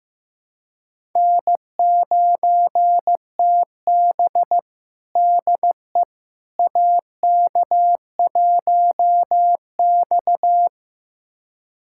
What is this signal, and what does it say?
Morse code, heard as a keyed tone, about 15 wpm, 700 Hz. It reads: N9TB DE AK1X
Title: USA Callsign DE Callsign - 4 Characters - 15wpm